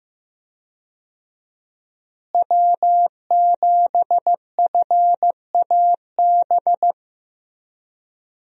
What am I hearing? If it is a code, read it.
Morse code, heard as a keyed tone, about 16 wpm, 700 Hz. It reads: W7FAB
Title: USA Call Signs - 5 Characters - Repeated 3x - 15wpm